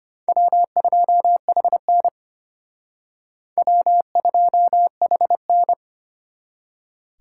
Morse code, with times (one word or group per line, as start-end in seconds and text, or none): 0.3-2.1 W2HN
3.6-5.7 W2HN